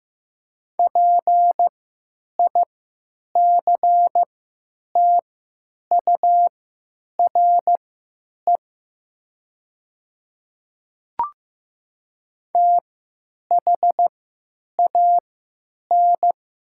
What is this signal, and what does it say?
Morse code, heard as a keyed tone, about 15 wpm, spaced as for 9 wpm, 700 Hz. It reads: PICTURE THAN
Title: 3x Character Spacing - Top 300 Words - 15wpm